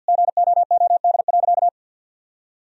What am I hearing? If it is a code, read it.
Morse code, heard as a keyed tone, about 50 wpm, 700 Hz. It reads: GOOD.
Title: Sentences from Top 1000 Words - Review Part 4 - 50wpm